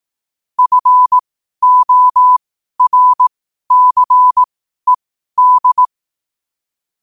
Morse code, heard as a keyed tone, about 18 wpm, spaced as for 12 wpm, 1000 Hz. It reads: FORCED
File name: qotdfiles / 12wpm